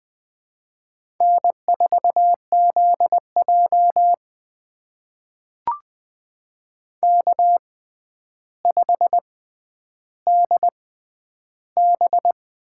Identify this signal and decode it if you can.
Morse code, heard as a keyed tone, about 20 wpm, spaced as for 10 wpm, 700 Hz. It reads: N4ZJ K5DB
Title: USA Call Signs - 6x Character Spacing - 20wpm